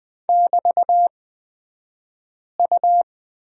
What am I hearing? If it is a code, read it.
Morse code, heard as a keyed tone, about 20 wpm, spaced as for 5 wpm, 700 Hz. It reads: <BT>U
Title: 223 Lesson 36d - BT - Two Characters 20wpm